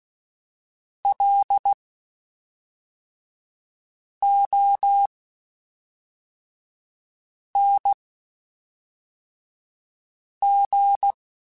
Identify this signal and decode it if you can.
Morse code, heard as a keyed tone, about 16 wpm, spaced as for 3 wpm, 800 Hz. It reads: LONG
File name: Quotes Spaced / Quotes Spaced-mp3s / Temper sp @5